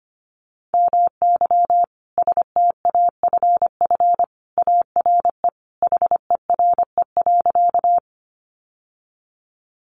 Morse code, heard as a keyed tone, about 25 wpm, 700 Hz. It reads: MY STAFF ARE HERE.
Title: Sentences from Top 1200 Words 25wpm